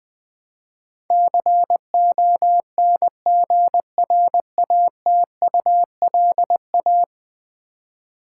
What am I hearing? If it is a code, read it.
Morse code, heard as a keyed tone, about 20 wpm, 700 Hz. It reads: CONGRATULA
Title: Build A Word - 5 Letters At A Time - 12-Letter Words - 20wpm